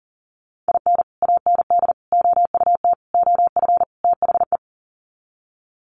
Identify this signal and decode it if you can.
Morse code, heard as a keyed tone, about 40 wpm, 700 Hz. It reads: IN AND OUT OF THE
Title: Sets of 5 Words 40wpm